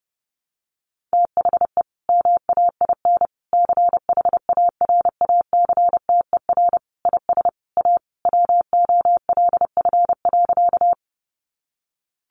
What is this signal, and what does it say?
Morse code, heard as a keyed tone, about 30 wpm, 700 Hz. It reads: THE MAIN CHARACTER IS A WOLF.
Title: Additional Easy Phrases 30wpm